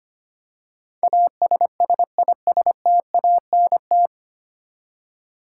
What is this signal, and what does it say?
Morse code, heard as a keyed tone, about 25 wpm, 700 Hz. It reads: ASSISTANT
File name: practice / 25wpm / 9 Letter Words - First 3 Letters Given - 25wpm